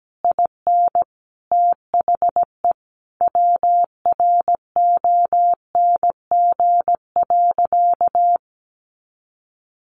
Morse code, heard as a keyed tone, about 17 wpm, 700 Hz. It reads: IN THE WRONG.
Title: Sentences from Top 1200 Words 17wpm